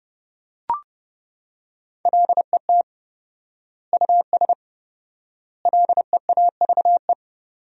Morse code, heard as a keyed tone, about 30 wpm, 700 Hz. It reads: LET US LEAVE
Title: 4x Word-Spacing - Sentences from Top 400 Words - 30wpm